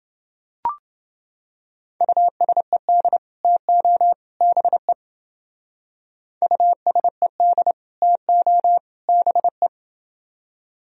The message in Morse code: USED TO BE USED TO BE

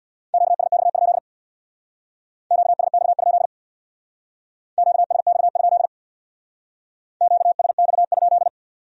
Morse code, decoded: QSX? QSX? QSX? QSX?

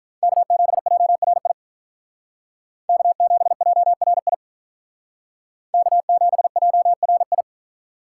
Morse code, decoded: K7JRI K7JRI K7JRI

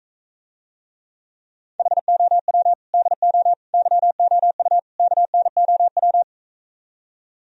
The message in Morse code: HOW DO YOU KNOW

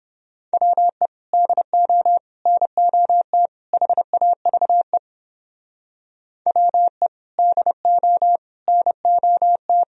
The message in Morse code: WE DO NOT HAVE WE DO NOT

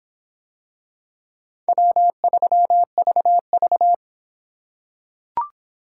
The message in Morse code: W3VV

